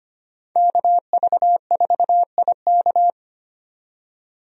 Morse code, heard as a keyed tone, about 25 wpm, 690 Hz. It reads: KV4IK
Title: USA Call Signs - 5 Characters - 25wpm